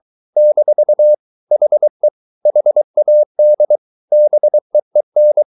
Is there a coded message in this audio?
Yes: Morse code from 0.4 to 5.4 s, heard as a keyed tone, about 23 wpm, 600 Hz. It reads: - HE HAD BEEN